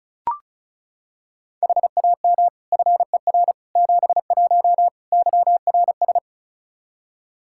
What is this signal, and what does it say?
Morse code, heard as a keyed tone, about 34 wpm, 700 Hz. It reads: HAM FER 71 YRS
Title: K1USN SST QSO Practice 35wpm